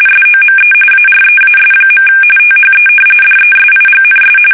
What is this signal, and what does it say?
RTTY, about 45.45 baud. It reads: SOME AREAS. THE DESIGNS
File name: fsk850 / battleshipnj